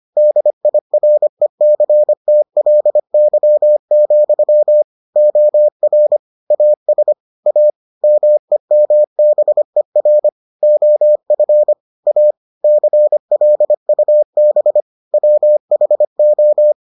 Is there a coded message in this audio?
Yes: Morse code from 0.2 to 16.7 s, heard as a keyed tone, about 25 wpm, 600 Hz. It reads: DIRECTLY, OR AS A MEMBER OF A CLUB WHO